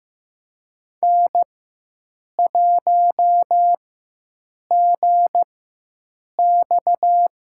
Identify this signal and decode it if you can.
Morse code, heard as a keyed tone, about 15 wpm, spaced as for 7 wpm, 700 Hz. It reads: N1GX
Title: USA Call Signs - 4 Characters - 4x Character Spacing - 15wpm